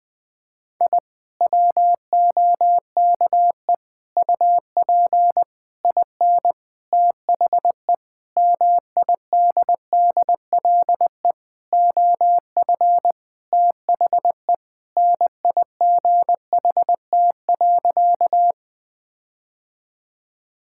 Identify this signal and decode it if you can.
Morse code, heard as a keyed tone, about 20 wpm, 700 Hz. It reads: I WOKE UP IN THE MIDDLE OF THE NIGHT.
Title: Sentences with 5-Word Phrases - 20wpm